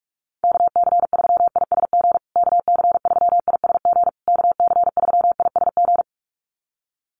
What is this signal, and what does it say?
Morse code, heard as a keyed tone, about 44 wpm, 690 Hz. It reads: KC3ISG KC3ISG KC3ISG
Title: USA Call Signs Repeated 3x 45wpm